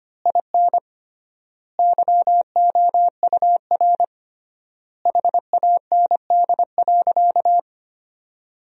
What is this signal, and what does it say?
Morse code, heard as a keyed tone, about 25 wpm, 700 Hz. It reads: IN YOUR HAND.
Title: 3x Word-Spacing - Sentences from Top 400 Words - 25wpm